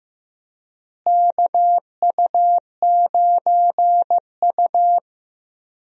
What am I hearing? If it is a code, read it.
Morse code, heard as a keyed tone, about 15 wpm, 700 Hz. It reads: KU9U
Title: USA Call Signs - 4 Characters - Repeated 3x - 15wpm